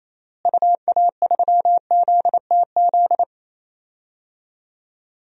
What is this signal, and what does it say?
Morse code, heard as a keyed tone, about 28 wpm, 700 Hz. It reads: UA3ZTZ
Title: DXCC-100 Call Signs - 6 Characters - Repeated 2x - 28wpm